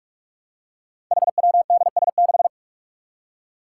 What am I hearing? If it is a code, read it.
Morse code, heard as a keyed tone, about 44 wpm, 700 Hz. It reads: HWDS6
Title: Five Challenging Letter-Number Combinations 45wpm